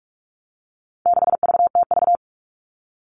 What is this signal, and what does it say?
Morse code, heard as a keyed tone, about 44 wpm, 700 Hz. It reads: 6VTV
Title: All Cut Numbers - 4-Digit 45wpm